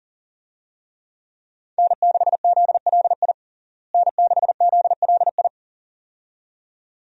Morse code, heard as a keyed tone, about 37 wpm, 700 Hz. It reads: N6ZLI N6ZLI
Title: USA Call Signs - 5 Characters - Repeated 2x - 40wpm